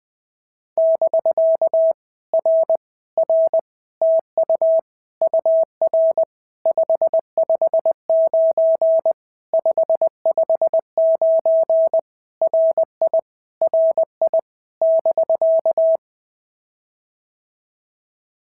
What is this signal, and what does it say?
Morse code, heard as a keyed tone, about 20 wpm, 650 Hz. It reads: <BK> R R TU UR 559 559 RI RI <BK>
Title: POTA QSO Practice - 20wpm